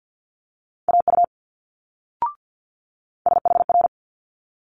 Morse code, heard as a keyed tone, about 50 wpm, 700 Hz. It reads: AU SHR
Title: Multi-letter Phonemes Quick Fire 50wpm